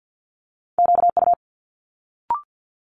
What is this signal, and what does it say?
Morse code, heard as a keyed tone, about 50 wpm, 700 Hz. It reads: QU
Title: Multi-letter Phonemes Quick Fire 50wpm